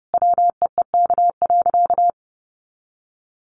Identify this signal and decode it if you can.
Morse code, heard as a keyed tone, about 30 wpm, 700 Hz. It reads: WEEK.
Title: Easy Phrases 30wpm